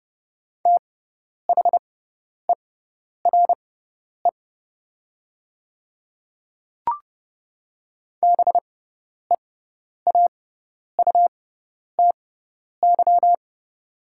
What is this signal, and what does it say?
Morse code, heard as a keyed tone, about 30 wpm, spaced as for 10 wpm, 700 Hz. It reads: THERE BEAUTY
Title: 6x Character Spacing - Top 500 Words - 30wpm